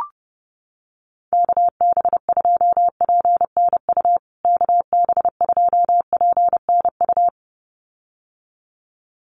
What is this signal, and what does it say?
Morse code, heard as a keyed tone, about 30 wpm, 700 Hz. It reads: KB2PNU KB2PNU
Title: USA Call Signs Repeated 2x 30wpm